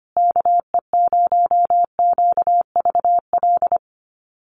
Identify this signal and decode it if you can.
Morse code, heard as a keyed tone, about 25 wpm, 700 Hz. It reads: KE0QVL
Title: USA Call Signs Repeated 2x 25wpm